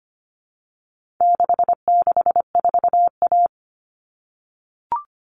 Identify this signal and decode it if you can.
Morse code, heard as a keyed tone, about 25 wpm, 700 Hz. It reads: 664A
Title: All Cut Numbers - 4-Digit 25wpm